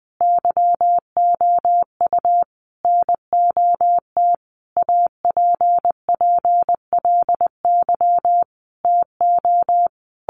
Morse code, heard as a keyed tone, about 20 wpm, 700 Hz. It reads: YOU NOT APPLY TO
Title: Sentences from Top 1200 Words 20wpm